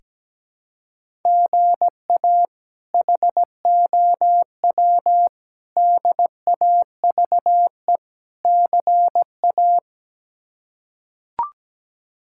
Morse code, heard as a keyed tone, about 17 wpm, 700 Hz. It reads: GA HOW DAVE CA
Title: K1USN SST QSO Practice - 17wpm